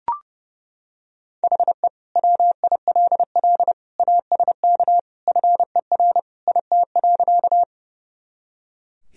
Morse code, heard as a keyed tone, about 30 wpm, 700 Hz. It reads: HE WILL ASK FER IT.
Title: Sentences from Top 200 Words and Abbreviations 30wpm